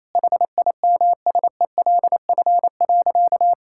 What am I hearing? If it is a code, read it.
Morse code, heard as a keyed tone, about 28 wpm, 700 Hz. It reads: HIMSELF.
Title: Sentences from Top 1600 Words 28wpm